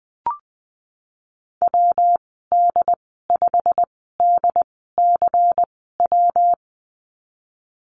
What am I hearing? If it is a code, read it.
Morse code, heard as a keyed tone, about 20 wpm, spaced as for 15 wpm, 700 Hz. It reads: WD5DCW
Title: 161 Lesson 26g - Letter g - Call Signs 20wpm